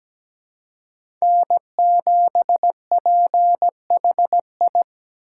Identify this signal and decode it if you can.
Morse code, heard as a keyed tone, about 17 wpm, 700 Hz. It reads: N7PHI
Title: K1USN SST QSO Practice - 17wpm